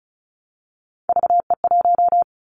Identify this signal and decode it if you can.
Morse code, heard as a keyed tone, about 34 wpm, 700 Hz. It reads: VE1
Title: Callsign Prefixes for Top 100 DXCC - 35wpm